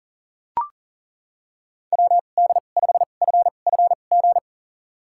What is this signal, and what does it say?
Morse code, heard as a keyed tone, about 38 wpm, spaced as for 28 wpm, 690 Hz. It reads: WD5FFG
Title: USA Call Signs - 6 Characters - 2x Character Spacing - 40wpm